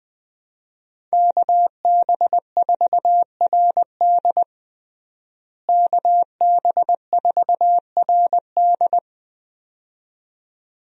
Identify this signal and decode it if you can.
Morse code, heard as a keyed tone, about 20 wpm, 700 Hz. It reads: KB4RD KB4RD
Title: USA Call Signs - 5 Characters - Repeated 2x - 20wpm